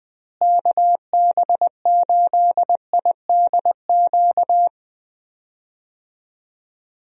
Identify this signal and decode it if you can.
Morse code, heard as a keyed tone, about 20 wpm, 700 Hz. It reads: KB8IDQ
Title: USA Call Signs - 6 Characters - Repeated 3x - 20wpm